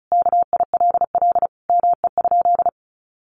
Morse code, heard as a keyed tone, about 34 wpm, 700 Hz. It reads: KILL ME?
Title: Sentences from Top 1400 Words 35wpm